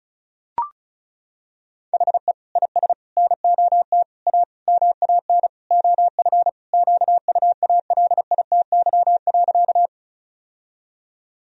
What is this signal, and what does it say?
Morse code, heard as a keyed tone, about 34 wpm, 700 Hz. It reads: HE IS NOT A MAN OF QUALITY.
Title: Sentences from Top 1000 Words - Review Part 4 - 35wpm